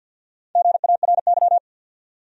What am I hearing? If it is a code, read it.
Morse code, heard as a keyed tone, about 52 wpm, 700 Hz. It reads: GARY
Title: USA Common Names - Max 5 Letters - 50wpm